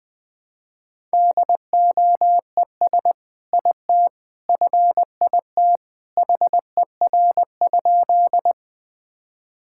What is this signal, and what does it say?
Morse code, heard as a keyed tone, about 20 wpm, 700 Hz. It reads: DOES IT FIT HER?
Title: Sentences from Top 1000 Words - Review Part 5 - 20wpm